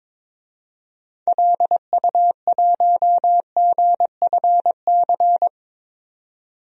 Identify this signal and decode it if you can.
Morse code, heard as a keyed tone, about 22 wpm, 700 Hz. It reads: LU1GFC
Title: DXCC-100 Call Signs - 6 Characters - 22wpm